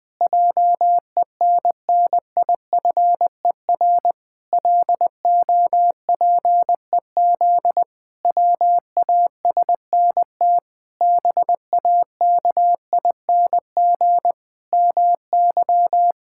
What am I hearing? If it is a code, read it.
Morse code, heard as a keyed tone, about 21 wpm, 700 Hz. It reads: JENNIFER LOPEZ WASNT BAKING MY
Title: Pangram Challenge v2 - 20wpm